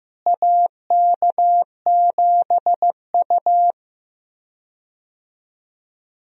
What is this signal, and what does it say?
Morse code, heard as a keyed tone, about 15 wpm, 700 Hz. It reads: AK7U
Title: USA Call Signs - 4 Characters - Repeated 2x - 15wpm